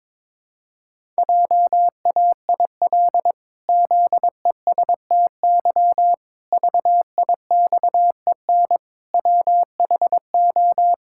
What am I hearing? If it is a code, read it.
Morse code, heard as a keyed tone, about 22 wpm, 700 Hz. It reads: JAIL ZESTY VIXEN WHO
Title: Pangram Challenge v2 - 22wpm